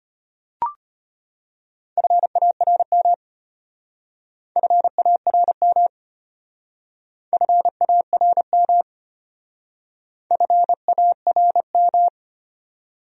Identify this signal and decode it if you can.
Morse code, heard as a keyed tone, about 32 wpm, 700 Hz. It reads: FARM FARM FARM FARM